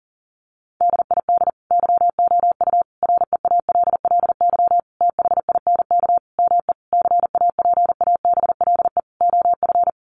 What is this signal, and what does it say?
Morse code, heard as a keyed tone, about 40 wpm, 700 Hz. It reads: DID YOU REALLY THINK ME CAPABLE OF
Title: Sentences from Top 1200 Words 40wpm